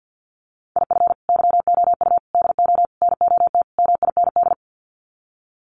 Morse code, heard as a keyed tone, about 48 wpm, 700 Hz. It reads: IF YOU DO NOT MIND
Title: Sets of 5 Words 50wpm